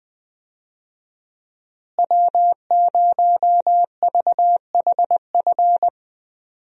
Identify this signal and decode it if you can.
Morse code, heard as a keyed tone, about 20 wpm, 700 Hz. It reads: W0VHF
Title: USA Call Signs - 5 Characters - Repeated 3x - 20wpm